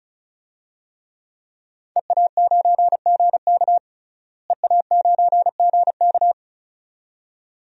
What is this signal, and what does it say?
Morse code, heard as a keyed tone, about 36 wpm, 700 Hz. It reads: EA9GK EA9GK
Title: DXCC-100 Call Signs - 5 Characters - Repeated 2x - 35wpm